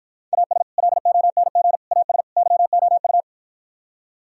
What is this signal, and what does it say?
Morse code, heard as a keyed tone, about 52 wpm, 700 Hz. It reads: AS LONG AS YOU